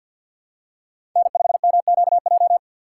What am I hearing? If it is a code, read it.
Morse code, heard as a keyed tone, about 52 wpm, 700 Hz. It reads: N5MQJ
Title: USA Call Signs - 4x Character Spacing - 50wpm